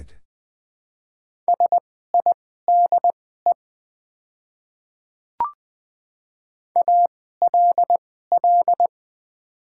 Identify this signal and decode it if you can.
Morse code, heard as a keyed tone, about 20 wpm, spaced as for 15 wpm, 690 Hz. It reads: SIDE ALL
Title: 2x Character Spacing - Top 100 Words - 20wpm